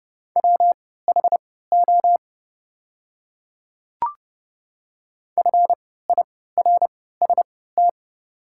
Morse code, heard as a keyed tone, about 29 wpm, spaced as for 17 wpm, 700 Hz. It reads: WHO FIRST
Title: 3x Character Spacing - Top 100 Words - 30wpm